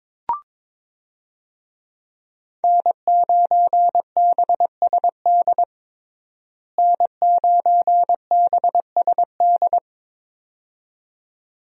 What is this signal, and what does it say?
Morse code, heard as a keyed tone, about 22 wpm, 700 Hz. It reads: N9BSD N9BSD
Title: USA Call Signs - 5 Characters - Repeated 2x - 22wpm